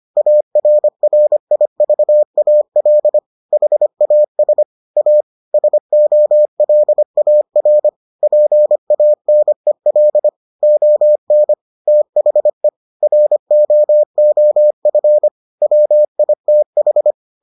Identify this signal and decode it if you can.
Morse code, heard as a keyed tone, about 25 wpm, 600 Hz. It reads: ARRIVAL HAS A SOLAR PANEL ON THE ROOF WITH